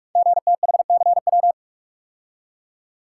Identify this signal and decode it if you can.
Morse code, heard as a keyed tone, about 44 wpm, 690 Hz. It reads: GTHCW
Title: Code Groups - 5 Letters - 45wpm